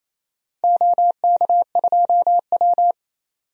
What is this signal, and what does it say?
Morse code, heard as a keyed tone, about 28 wpm, 700 Hz. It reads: OK2W
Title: DXCC-100 Call Signs - 4 Characters - 28wpm